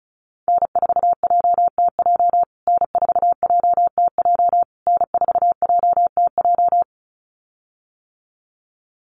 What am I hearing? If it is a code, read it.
Morse code, heard as a keyed tone, about 34 wpm, 700 Hz. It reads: N4JTJ N4JTJ N4JTJ
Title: USA Call Signs Repeated 3x 35wpm